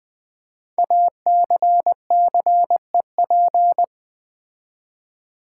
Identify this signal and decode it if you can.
Morse code, heard as a keyed tone, about 20 wpm, 700 Hz. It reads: ACCEP
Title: Build A Word - 5 Letters At A Time - 10-Letter Words - 20wpm